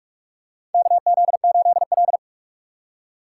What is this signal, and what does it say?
Morse code, heard as a keyed tone, about 46 wpm, 700 Hz. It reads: KZ8L
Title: USA Call Signs - 4 Characters - 4x Character Spacing - 45wpm